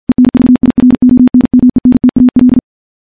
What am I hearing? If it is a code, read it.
Morse code, heard as a keyed tone, about 48 wpm, 250 Hz. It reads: ENVIRONMENTAL
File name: morse / morse-training-50WPM